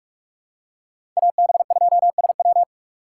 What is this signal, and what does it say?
Morse code, heard as a keyed tone, about 42 wpm, 700 Hz. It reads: AB2SW
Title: USA Call Signs - 6x Character Spacing - 45wpm